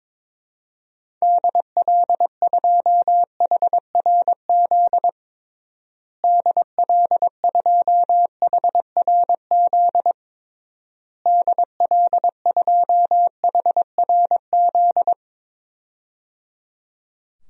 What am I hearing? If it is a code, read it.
Morse code, heard as a keyed tone, about 22 wpm, 700 Hz. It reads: DL2HRZ DL2HRZ DL2HRZ
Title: DXCC-100 Call Signs - 6 Characters - Repeated 3x - 22wpm